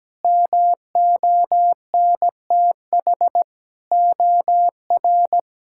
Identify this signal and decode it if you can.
Morse code, heard as a keyed tone, about 17 wpm, 700 Hz. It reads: MONTH OR
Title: Sentences from Top 1000 Words - Review Part 5 - 17wpm